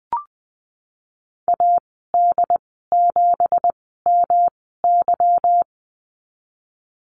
Morse code, heard as a keyed tone, about 20 wpm, 700 Hz. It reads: A D 7 M Y
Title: 207 Lesson 33g - Letter j - Call Signs 20wpm